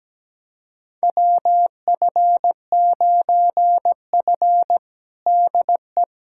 Morse code, heard as a keyed tone, about 17 wpm, 700 Hz. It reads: WF9F DE
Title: USA Callsign DE Callsign - 4 Characters - 17wpm